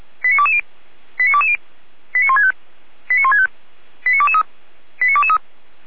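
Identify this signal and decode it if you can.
Five-tone selective call (ZVEI), heard as quick runs of five tones, tones of about 70 ms. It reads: 89220 89220 89165 89165 89223 89223
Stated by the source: Five-tone ZVEI selective call, code 89010, received by SDR, from feuerwehr-orlamuende (Orlamünde fire brigade).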